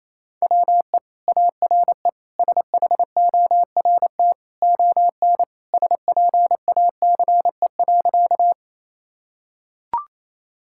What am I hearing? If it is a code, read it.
Morse code, heard as a keyed tone, about 27 wpm, 700 Hz. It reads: WE ARE SHORT ON SPACE.
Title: Sentences from Top 1000 Words - Review Part 5 - 28wpm